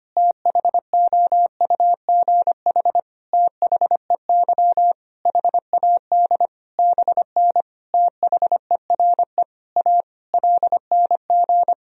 Morse code, heard as a keyed tone, about 25 wpm, 700 Hz. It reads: THOUGH THEY HAD BN THERE A LNG
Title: Sentences from Top 400 Words and Abbreviations 25wpm